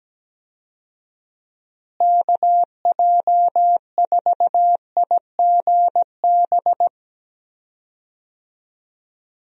Morse code, heard as a keyed tone, about 17 wpm, 700 Hz. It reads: KJ4IGB
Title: USA Call Signs - 6 Characters - Repeated 3x - 17wpm